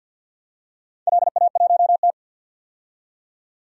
Morse code, heard as a keyed tone, about 53 wpm, 700 Hz. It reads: LA1T
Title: DXCC-100 Call Signs - 4 Characters - 50wpm